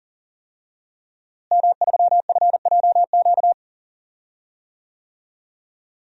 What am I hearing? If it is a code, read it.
Morse code, heard as a keyed tone, about 39 wpm, 700 Hz. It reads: M3FJQ
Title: DXCC-100 Call Signs - 5 Characters - Repeated 2x - 40wpm